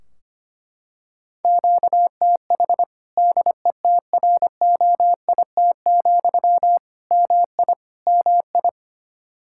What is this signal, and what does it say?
Morse code, heard as a keyed tone, about 25 wpm, 700 Hz. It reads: QTH DETROIT, MI MI
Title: Practice QSO - Exchange 1 - Beginning - 25wpm